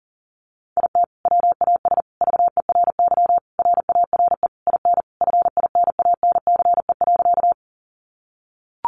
Morse code, heard as a keyed tone, about 40 wpm, 700 Hz. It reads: IT WAS VERY RARE IN FINANCE.
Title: Sentences from Top 1500 Words 40wpm